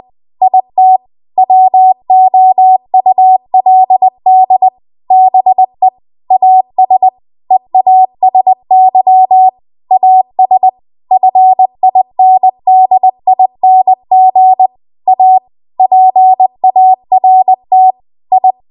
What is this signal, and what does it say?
Morse code, heard as a keyed tone, about 20 wpm, 750 Hz. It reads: IT WOULD BE AS EASY AS FINDING A PART I
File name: Archive / 20 WPM / 20WPM